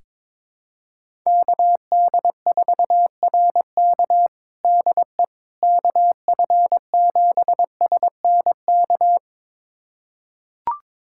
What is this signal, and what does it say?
Morse code, heard as a keyed tone, about 21 wpm, 700 Hz. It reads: KD4RK DE KF7SNK
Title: USA Callsign DE Callsign 22wpm